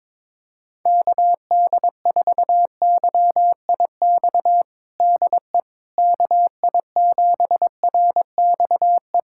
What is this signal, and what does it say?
Morse code, heard as a keyed tone, about 22 wpm, 700 Hz. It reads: KD4YIX DE KI7RXE
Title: USA Callsign DE Callsign - 6 Characters - 22wpm